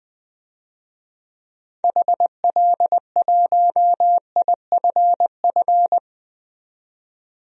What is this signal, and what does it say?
Morse code, heard as a keyed tone, about 20 wpm, 700 Hz. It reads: HL1IFF
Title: DXCC-100 Call Signs - 6 Characters - 20wpm